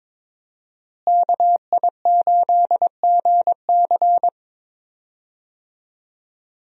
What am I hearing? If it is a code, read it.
Morse code, heard as a keyed tone, about 22 wpm, 700 Hz. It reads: KI8GC
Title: USA Call Signs - 5 Characters - Repeated 3x - 22wpm